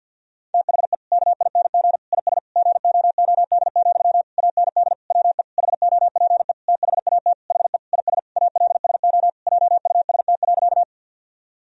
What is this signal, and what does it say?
Morse code, heard as a keyed tone, about 48 wpm, 700 Hz. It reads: THE KING IS GOOD, AND WE HOPE THAT HE IS ALSO JUST.
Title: Sentences from Top 1000 Words - Review Part 1 - 50wpm